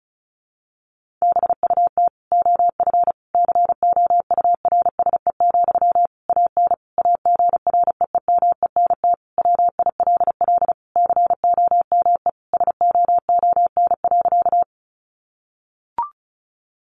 Morse code, heard as a keyed tone, about 34 wpm, 700 Hz. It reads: BUT OF COURSE, AN AGREEMENT WILL COME SOON.
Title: Sentences from Top 1500 Words 35wpm